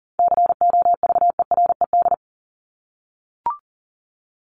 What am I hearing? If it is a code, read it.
Morse code, heard as a keyed tone, about 40 wpm, 700 Hz. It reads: COVERED
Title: ED Suffix - Top 100 Words with Base Words 40wpm